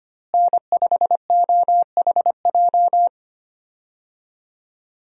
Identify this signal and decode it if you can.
Morse code, heard as a keyed tone, about 25 wpm, 700 Hz. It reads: N5OHJ
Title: USA Call Signs - 5 Characters - Repeated 3x - 25wpm